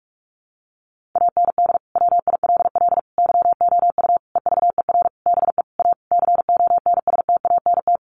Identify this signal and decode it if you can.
Morse code, heard as a keyed tone, about 44 wpm, 700 Hz. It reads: AND WILL YOU EVER BE A CONSTANT
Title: Sentences from Top 1500 Words 45wpm